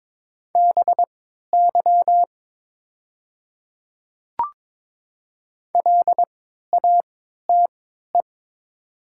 Morse code, heard as a keyed tone, about 22 wpm, spaced as for 12 wpm, 700 Hz. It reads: BY LATE
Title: 3x Character Spacing - Top 300 Words - 22wpm